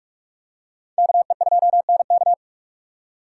Morse code, heard as a keyed tone, about 41 wpm, 700 Hz. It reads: KE2NK